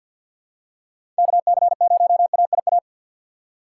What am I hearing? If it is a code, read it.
Morse code, heard as a keyed tone, about 49 wpm, 700 Hz. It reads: KC0AIA
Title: USA Call Signs - 6 Characters - 4x Character Spacing - 50wpm